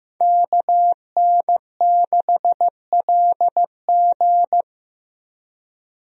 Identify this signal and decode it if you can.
Morse code, heard as a keyed tone, about 15 wpm, 700 Hz. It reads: KN6LG
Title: USA Call Signs - 5 Characters - Repeated 2x - 15wpm